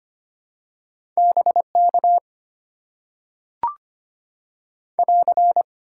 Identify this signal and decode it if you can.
Morse code, heard as a keyed tone, about 25 wpm, 700 Hz. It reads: BK <AR>
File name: Top 90 QSO Elements - 6x Character Spacing - 25wpm